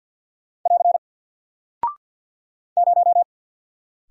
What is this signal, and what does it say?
Morse code, heard as a keyed tone, about 49 wpm, 700 Hz. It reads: <AR> 0
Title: Single Character (Letter:Number:Punctuation:Prosign) 50wpm